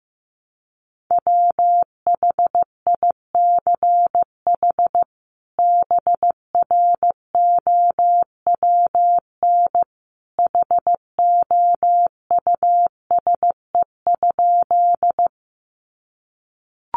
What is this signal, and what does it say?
Morse code, heard as a keyed tone, about 15 wpm, 700 Hz. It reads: WHICH BROWN HOUSE?
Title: Sentences from Top 1200 Words 15wpm